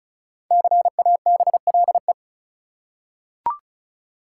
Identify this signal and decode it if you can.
Morse code, heard as a keyed tone, about 34 wpm, 700 Hz. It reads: CABLE
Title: Five Letter Words - 35wpm